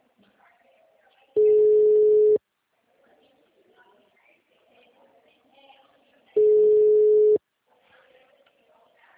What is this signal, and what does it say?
Ringback tone (Europe, 425 Hz), with speech.
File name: out- -2004- - - .54756